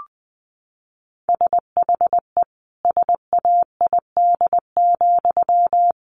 Morse code, heard as a keyed tone, about 20 wpm, 700 Hz. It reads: SHE SAID,
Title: Sentences from Top 800 Words 20wpm